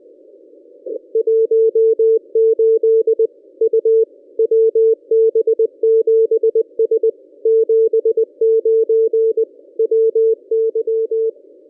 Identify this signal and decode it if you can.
Morse code, heard as a keyed tone, about 20 wpm, 450 Hz. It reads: 18 U WB7S 79 WY
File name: contests / ss-cw-k5tr-op-n5rz -1200